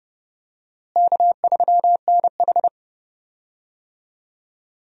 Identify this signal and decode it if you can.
Morse code, heard as a keyed tone, about 30 wpm, 700 Hz. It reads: K3NH
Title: USA Call Signs - 4 Characters - Repeated 3x - 30wpm